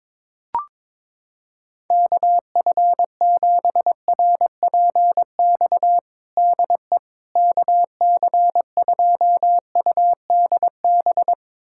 Morse code, heard as a keyed tone, about 22 wpm, 700 Hz. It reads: KF7RPX DE KC2UDB
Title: USA Callsign DE Callsign - 6 Characters - 22wpm